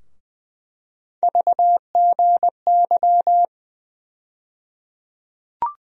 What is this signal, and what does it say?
Morse code, heard as a keyed tone, about 20 wpm, 700 Hz. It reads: VGY
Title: USA Call Sign Postfixes - 3 Letters 20wpm